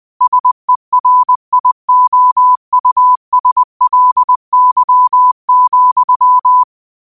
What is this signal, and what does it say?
Morse code, heard as a keyed tone, about 20 wpm, 1000 Hz. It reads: SERIOUSLY,